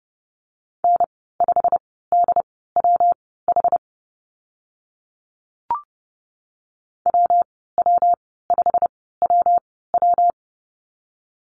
Morse code, heard as a keyed tone, about 32 wpm, spaced as for 17 wpm, 700 Hz. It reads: N5DWH WW5WW
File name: Lesson 19g - Letter w - Call Signs 30wpm